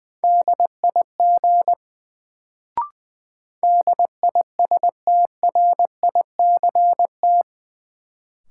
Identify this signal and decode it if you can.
Morse code, heard as a keyed tone, about 20 wpm, 700 Hz. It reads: DIG DISTRICT
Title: Sentences from Top 1600 Words - New Word Drill Alt - 20wpm